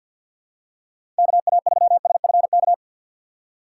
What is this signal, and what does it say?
Morse code, heard as a keyed tone, about 52 wpm, 700 Hz. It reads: KA3SFK
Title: USA Call Signs - 4x Character Spacing - 50wpm